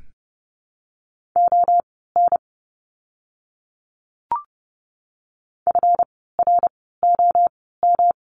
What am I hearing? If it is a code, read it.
Morse code, heard as a keyed tone, about 30 wpm, spaced as for 17 wpm, 700 Hz. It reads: ON FROM
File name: course / 30:10wpm / 230 Lesson 37f - Letter x - Words 30wpm